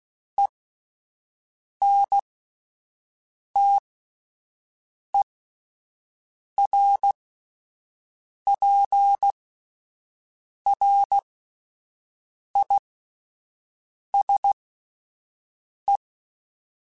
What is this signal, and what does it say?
Morse code, heard as a keyed tone, about 16 wpm, spaced as for 5 wpm, 800 Hz. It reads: ENTERPRISE
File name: fromSMs / FromSMs mp3s / Cities@5